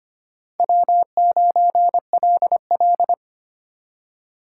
Morse code, heard as a keyed tone, about 32 wpm, 700 Hz. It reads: W9LL